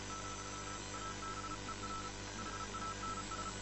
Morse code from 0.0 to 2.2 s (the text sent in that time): ,ABER